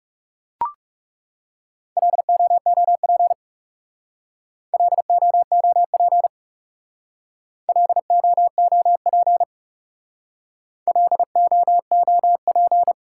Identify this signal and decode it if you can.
Morse code, heard as a keyed tone, about 38 wpm, 700 Hz. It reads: LOOP LOOP LOOP LOOP